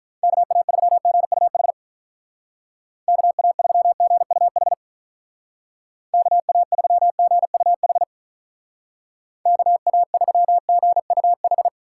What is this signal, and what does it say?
Morse code, heard as a keyed tone, about 43 wpm, 700 Hz. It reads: KA3GUH KA3GUH KA3GUH KA3GUH
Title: Speed Racing - USA 2x3 Call Signs 35wpm